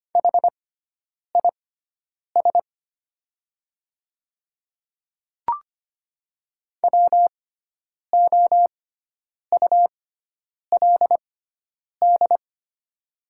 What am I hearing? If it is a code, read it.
Morse code, heard as a keyed tone, about 25 wpm, spaced as for 9 wpm, 700 Hz. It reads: HIS WOULD
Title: 6x Character Spacing - Top 100 Words - 25wpm